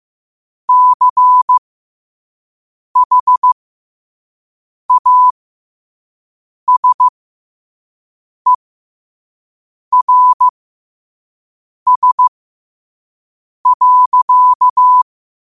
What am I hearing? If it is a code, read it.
Morse code, heard as a keyed tone, about 15 wpm, spaced as for 5 wpm, 1000 Hz. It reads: CHASERS.
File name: qotdfiles / 5wpm